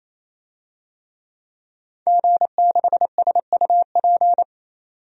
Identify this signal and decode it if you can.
Morse code, heard as a keyed tone, about 28 wpm, 700 Hz. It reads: G6SUP
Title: DXCC-100 Call Signs - 5 Characters - 28wpm